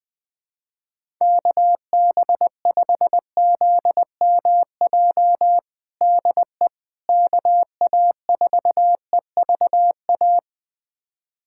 Morse code, heard as a keyed tone, about 20 wpm, 700 Hz. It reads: KB5ZMJ DE KA4EVA
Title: USA Callsign DE Callsign - 6 Characters - 20wpm